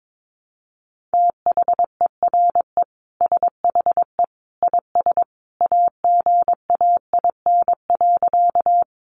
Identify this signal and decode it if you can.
Morse code, heard as a keyed tone, about 22 wpm, 700 Hz. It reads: THERE SHE IS AGAIN.
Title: Sentences from Top 200 Words 22wpm